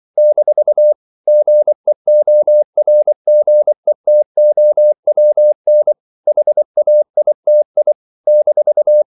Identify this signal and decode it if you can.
Morse code, heard as a keyed tone, about 24 wpm, 600 Hz. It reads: - GEORGETOWN HAITI -
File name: CW / Practice / NationalCapitals24